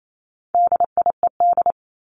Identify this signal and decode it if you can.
Morse code, heard as a keyed tone, about 28 wpm, 700 Hz. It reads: DIED
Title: ED Suffix - Top 100 Words with Base Words 28wpm